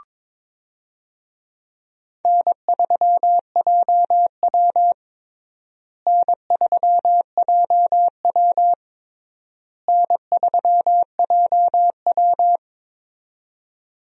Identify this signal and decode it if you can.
Morse code, heard as a keyed tone, about 21 wpm, 700 Hz. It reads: N3JW N3JW N3JW
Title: USA Call Signs - 4 Characters - Repeated 3x - 22wpm